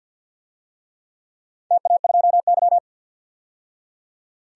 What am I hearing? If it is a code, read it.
Morse code, heard as a keyed tone, about 53 wpm, 700 Hz. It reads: TA2Y
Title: DXCC-100 Call Signs - 4 Characters - 50wpm